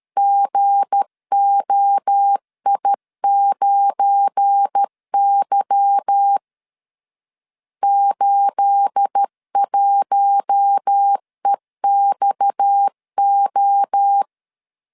Morse code, heard as a keyed tone, about 13 wpm, 800 Hz. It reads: GOI9Y 81EXO
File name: CW / DET / 1 9 1500C S B 12wpm 3819